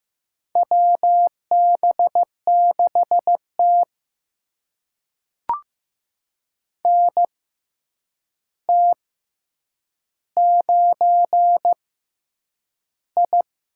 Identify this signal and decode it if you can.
Morse code, heard as a keyed tone, about 15 wpm, spaced as for 8 wpm, 700 Hz. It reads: WB6T NT9I
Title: USA Call Signs - 6x Character Spacing - 15wpm